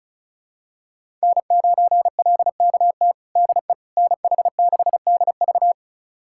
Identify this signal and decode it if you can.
Morse code, heard as a keyed tone, about 36 wpm, 700 Hz. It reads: N9LKT DE NH6DV
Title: USA Callsign DE Callsign - 5 Characters - 35wpm